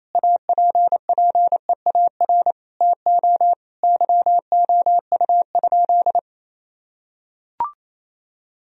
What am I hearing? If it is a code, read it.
Morse code, heard as a keyed tone, about 28 wpm, 700 Hz. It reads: APPEAR TO YOU?
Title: Sentences from Top 1000 Words - Review Part 2 - 28wpm